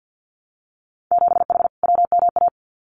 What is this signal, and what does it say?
Morse code, heard as a keyed tone, about 50 wpm, 700 Hz. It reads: 7H WMA
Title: Winter Field Day Exchanges Encore 50wpm